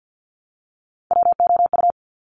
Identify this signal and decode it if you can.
Morse code, heard as a keyed tone, about 50 wpm, 700 Hz. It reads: WOU
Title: Top Three Letter Combinations in Words 50wpm